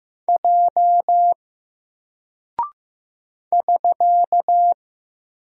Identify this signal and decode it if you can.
Morse code, heard as a keyed tone, about 15 wpm, 700 Hz. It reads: J <SK>
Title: Single Character (Letter:Number:Punctuation:Prosign) 15wpm